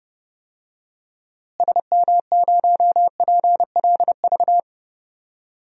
Morse code, heard as a keyed tone, about 30 wpm, 700 Hz. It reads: SM0PLV
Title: DXCC-100 Call Signs - 6 Characters - Repeated 2x - 30wpm